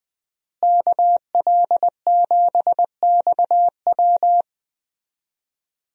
Morse code, heard as a keyed tone, about 20 wpm, 700 Hz. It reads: KL7XW
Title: DXCC-100 Call Signs - 5 Characters - Repeated 2x - 20wpm